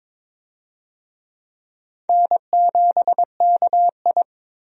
Morse code, heard as a keyed tone, about 22 wpm, 700 Hz. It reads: N7KI
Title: USA Call Signs - 4 Characters - 22wpm